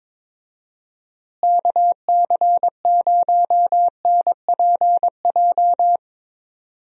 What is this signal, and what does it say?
Morse code, heard as a keyed tone, about 22 wpm, 700 Hz. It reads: KC0NPJ
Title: USA Call Signs - 6 Characters - Repeated 3x - 22wpm